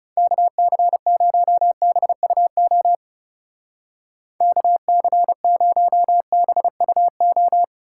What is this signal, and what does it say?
Morse code, heard as a keyed tone, about 39 wpm, 700 Hz. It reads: KC0BUO KC0BUO